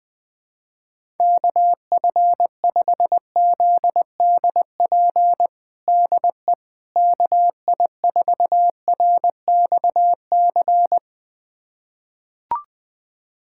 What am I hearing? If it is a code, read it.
Morse code, heard as a keyed tone, about 20 wpm, 700 Hz. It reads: KF5ZDP DE KI4RXC
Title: USA Callsign DE Callsign 20wpm